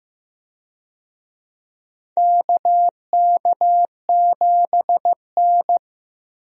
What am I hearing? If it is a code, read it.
Morse code, heard as a keyed tone, about 15 wpm, 700 Hz. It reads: KK7N
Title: USA Call Signs - 4 Characters - Repeated 3x - 15wpm